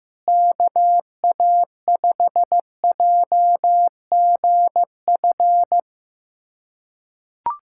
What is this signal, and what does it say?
Morse code, heard as a keyed tone, about 15 wpm, 700 Hz. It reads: KA5JGF
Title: USA Callsign DE Callsign - 6 Characters - 15wpm